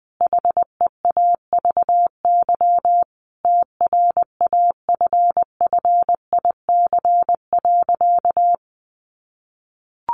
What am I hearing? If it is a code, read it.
Morse code, heard as a keyed tone, about 20 wpm, 700 Hz. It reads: HEAVY TRAFFIC.
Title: Sentences from Top 1300 Words 20wpm